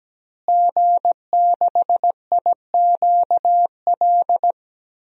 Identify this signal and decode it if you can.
Morse code, heard as a keyed tone, about 17 wpm, 700 Hz. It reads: G6IQL
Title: DXCC-100 Call Signs - 5 Characters - Repeated 2x - 17wpm